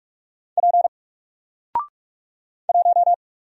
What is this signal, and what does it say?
Morse code, heard as a keyed tone, about 44 wpm, 700 Hz. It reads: P 1
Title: Single Letter-Number 45wpm